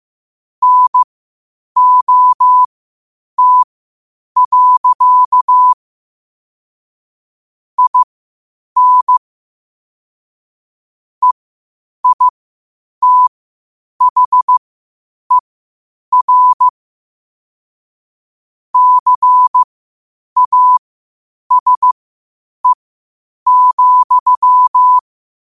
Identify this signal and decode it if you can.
Morse code, heard as a keyed tone, about 15 wpm, spaced as for 8 wpm, 1000 Hz. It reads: NOT. IN EITHER CASE,
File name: qotdfiles / 8wpm